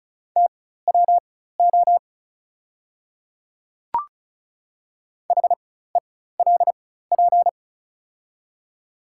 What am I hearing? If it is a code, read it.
Morse code, heard as a keyed tone, about 36 wpm, spaced as for 16 wpm, 700 Hz. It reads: TWO HELP
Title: 4x Character Spacing - Top 200 Words - 35wpm